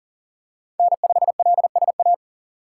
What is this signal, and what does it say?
Morse code, heard as a keyed tone, about 40 wpm, 700 Hz. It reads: N5LSA
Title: USA Call Signs - 6x Character Spacing - 40wpm